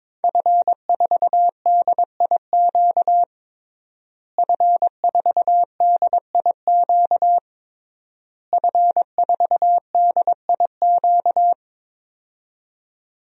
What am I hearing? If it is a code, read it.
Morse code, heard as a keyed tone, about 22 wpm, 700 Hz. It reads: F4DIQ F4DIQ F4DIQ
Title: DXCC-100 Call Signs - 5 Characters - Repeated 3x - 22wpm